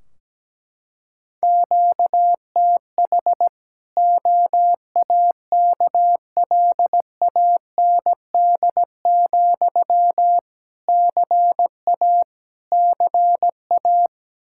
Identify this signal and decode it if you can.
Morse code, heard as a keyed tone, about 17 wpm, 700 Hz. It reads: QTH OAKLAND, CA CA
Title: Practice QSO - Exchange 1 - Beginning - 17wpm